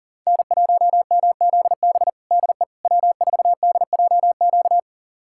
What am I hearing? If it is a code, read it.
Morse code, heard as a keyed tone, about 40 wpm, 700 Hz. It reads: N1MZB DE W4DJQ
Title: USA Callsign DE Callsign - 5 Characters - 40wpm